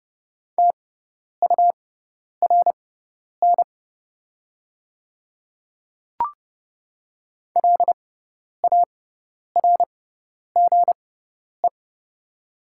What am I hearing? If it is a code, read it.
Morse code, heard as a keyed tone, about 30 wpm, spaced as for 10 wpm, 690 Hz. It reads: TURN LARGE
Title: 6x Character Spacing - Top 500 Words - 30wpm